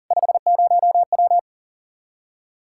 Morse code, heard as a keyed tone, about 40 wpm, 700 Hz. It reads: 50W
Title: Practice QSO - Exchange 2 - Info - 40wpm